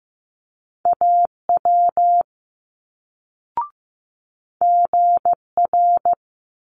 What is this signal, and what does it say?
Morse code, heard as a keyed tone, about 15 wpm, 700 Hz. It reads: AW GR
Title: Multi-letter Phonemes Quick Fire 15wpm